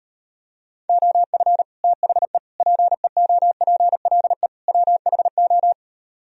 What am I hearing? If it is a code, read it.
Morse code, heard as a keyed tone, about 38 wpm, 700 Hz. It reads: OF THE PEOPLE WHO